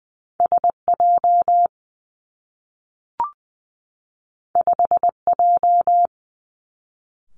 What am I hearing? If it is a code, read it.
Morse code, heard as a keyed tone, about 20 wpm, 700 Hz. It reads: S J 5 J
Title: Two Challenging Letter-Number Combinations 20wpm